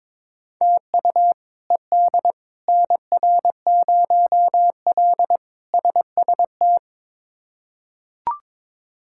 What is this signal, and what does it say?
Morse code, heard as a keyed tone, about 22 wpm, 700 Hz. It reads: TU ED NR0L SST
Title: K1USN SST QSO Practice - 22wpm